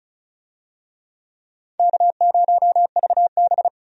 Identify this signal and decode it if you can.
Morse code, heard as a keyed tone, about 34 wpm, 700 Hz. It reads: K0VB
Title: USA Call Signs - 4 Characters - 35wpm